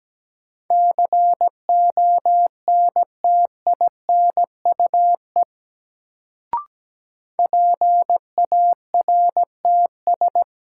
Morse code, heard as a keyed tone, about 17 wpm, 700 Hz. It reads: CONTINUE PARTS
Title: Sentences from Top 1000 Words - Review Part 6 - New Word Drill - 17wpm